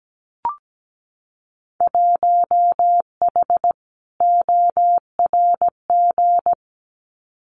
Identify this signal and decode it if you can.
Morse code, heard as a keyed tone, about 17 wpm, 700 Hz. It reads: 1H ORG
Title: Winter Field Day Exchanges 17wpm